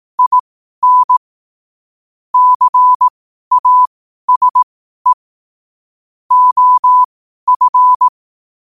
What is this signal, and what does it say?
Morse code, heard as a keyed tone, about 18 wpm, spaced as for 12 wpm, 1000 Hz. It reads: IN CASE OF
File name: qotdfiles / 12wpm